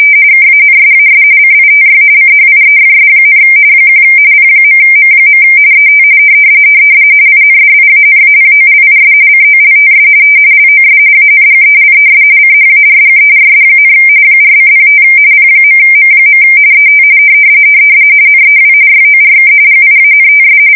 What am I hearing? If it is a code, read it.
RTTY, about 45.45 baud. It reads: KANSAS CITY CHIEFS, 1 P.M. ET INDIANAPOLIS COLTS LOS ANGELES CHARGERS, 4:05 P.M. ET NEW YORK GIANTS
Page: 170 Shift 2125/2295